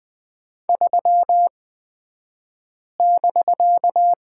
Morse code, heard as a keyed tone, about 20 wpm, spaced as for 5 wpm, 700 Hz. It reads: 3<BK>
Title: 249 Lesson 40d - BK - Two Characters 20wpm